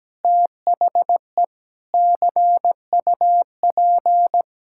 Morse code, heard as a keyed tone, about 17 wpm, 700 Hz. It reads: THE CUP
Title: Sentences from Top 1600 Words 17wpm